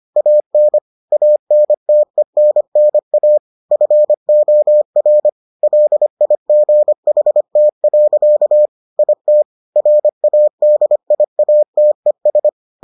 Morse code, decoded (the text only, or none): AN ANTENNA FOR LIGHT. IT RADIATES